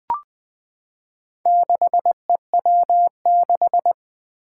Morse code, heard as a keyed tone, about 20 wpm, 700 Hz. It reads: 6EW6
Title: Four Challenging Letter-Number Combinations 20wpm